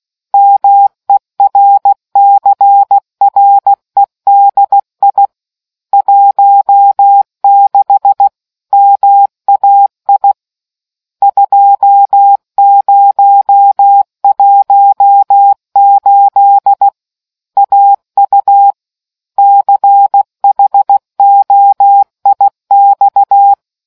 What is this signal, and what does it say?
Morse code, heard as a keyed tone, about 16 wpm, 800 Hz. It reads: MERCREDI 16 MAI 2018 AU CHOIX